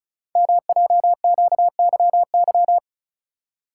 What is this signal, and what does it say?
Morse code, heard as a keyed tone, about 36 wpm, 700 Hz. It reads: MJQYY
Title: Code Groups - 5 Letters - 35wpm